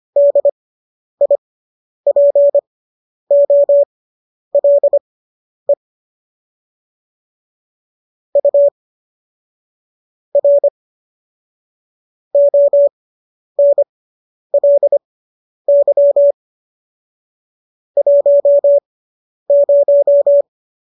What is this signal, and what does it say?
Morse code, heard as a keyed tone, about 25 wpm, spaced as for 10 wpm, 570 Hz. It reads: DIPOLE U R ONLY 10